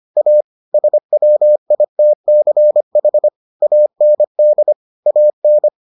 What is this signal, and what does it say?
Morse code, heard as a keyed tone, about 25 wpm, 600 Hz. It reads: A SWITCH AND AN